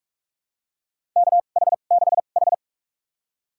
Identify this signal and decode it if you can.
Morse code, heard as a keyed tone, about 44 wpm, spaced as for 32 wpm, 690 Hz. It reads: KH6H
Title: USA Call Signs - 4 Characters - 2x Character Spacing - 45wpm